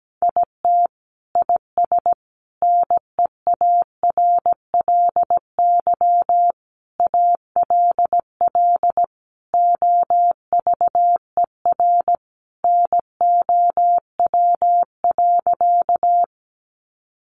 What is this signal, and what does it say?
Morse code, heard as a keyed tone, about 17 wpm, 700 Hz. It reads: IT IS NEARLY ALL OVER NOW.
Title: Sentences from Top 1300 Words 17wpm